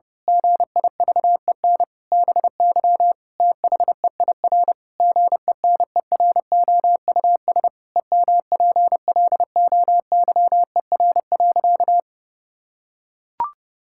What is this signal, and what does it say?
Morse code, heard as a keyed tone, about 30 wpm, 700 Hz. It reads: GIVEN BY THEIR GENEROUS EMPLOYER.
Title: Pangram Challenge v2 - 30wpm